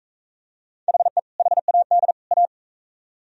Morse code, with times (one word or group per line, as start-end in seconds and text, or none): 0.9-1.2 HE
1.4-2.1 HAD
2.3-2.5 A